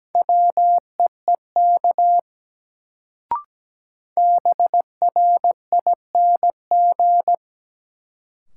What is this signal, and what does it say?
Morse code, heard as a keyed tone, about 17 wpm, 700 Hz. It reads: WEEK BRING
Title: Sentences from Top 500 Words - New Word Drill - 17wpm